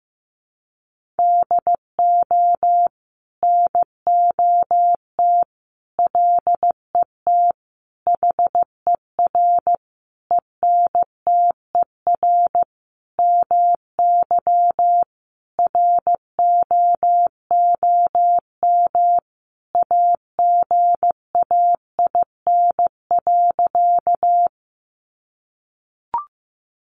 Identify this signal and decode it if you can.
Morse code, heard as a keyed tone, about 15 wpm, 700 Hz. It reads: DO NOT LET HER ENTER MY ROOM AGAIN.
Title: Sentences from Top 1300 Words 15wpm